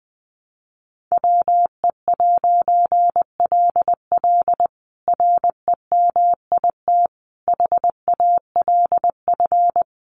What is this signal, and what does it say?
Morse code, heard as a keyed tone, about 20 wpm, 700 Hz. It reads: WE'LL REMIT HALF
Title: A Sound Of Thunder 20wpm